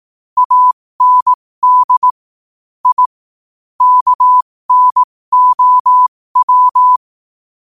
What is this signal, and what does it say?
Morse code, heard as a keyed tone, about 18 wpm, spaced as for 15 wpm, 1000 Hz. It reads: AND I KNOW
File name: qotdfiles / 15wpm